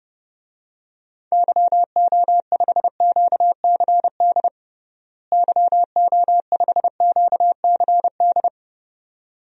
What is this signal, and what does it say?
Morse code, heard as a keyed tone, about 30 wpm, 700 Hz. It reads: YO5QCD YO5QCD
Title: DXCC-100 Call Signs - 6 Characters - Repeated 2x - 30wpm